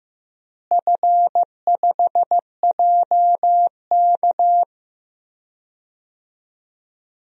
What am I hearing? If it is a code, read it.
Morse code, heard as a keyed tone, about 15 wpm, 700 Hz. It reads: F5JK
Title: DXCC-100 Call Signs - 4 Characters - Repeated 3x - 15wpm